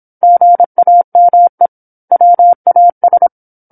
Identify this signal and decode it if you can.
Morse code, heard as a keyed tone, about 27 wpm, 700 Hz. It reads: GAME WAS
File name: 08-Chapter-08 26 WPM